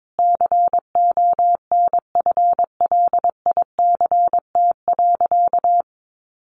Morse code, heard as a keyed tone, about 22 wpm, 700 Hz. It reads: CONFLICT.
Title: Sentences from Top 1300 Words 22wpm